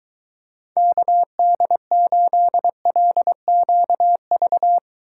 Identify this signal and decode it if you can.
Morse code, heard as a keyed tone, about 23 wpm, 700 Hz. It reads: KD8LQV